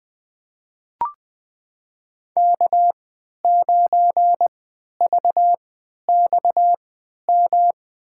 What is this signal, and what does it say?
Morse code, heard as a keyed tone, about 20 wpm, spaced as for 11 wpm, 700 Hz. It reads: K9VXM
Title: USA Call Signs - 5 Characters - 3x Character Spacing - 20wpm